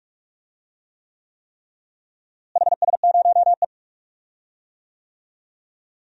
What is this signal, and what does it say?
Morse code, heard as a keyed tone, about 45 wpm, 700 Hz. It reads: HS0E
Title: DXCC-100 Call Signs - 4 Characters - Repeated 3x - 45wpm